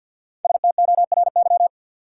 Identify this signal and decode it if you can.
Morse code, heard as a keyed tone, about 52 wpm, 700 Hz. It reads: STORY
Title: Sentences from Top 300 Words - New Word Drill - 50wpm